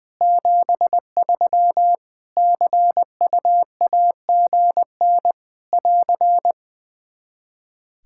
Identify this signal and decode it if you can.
Morse code, heard as a keyed tone, about 21 wpm, 700 Hz. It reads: 73 CUAGN <AR>
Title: Practice QSO - Exchange 4 - Ending - 20wpm